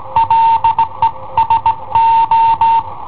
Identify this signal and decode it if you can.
Morse code, heard as a keyed tone, about 17 wpm, 910 Hz. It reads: LE SO
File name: i7ohp / cassetta i7ohp B